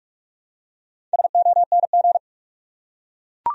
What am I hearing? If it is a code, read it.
Morse code, heard as a keyed tone, about 43 wpm, 700 Hz. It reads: SONG